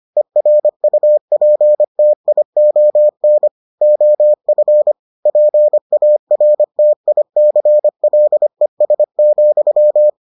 Morse code, heard as a keyed tone, about 25 wpm, 600 Hz. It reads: ERUPTION OF PARTICLES,